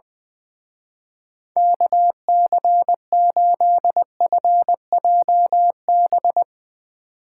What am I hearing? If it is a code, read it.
Morse code, heard as a keyed tone, about 20 wpm, 700 Hz. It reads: KC8FJB
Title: Speed Racing - USA 2x3 Call Signs 20wpm